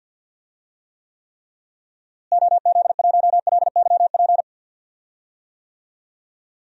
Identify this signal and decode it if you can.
Morse code, heard as a keyed tone, about 52 wpm, 700 Hz. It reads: OZ1LYP
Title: DXCC-100 Call Signs - 6 Characters - Repeated 3x - 50wpm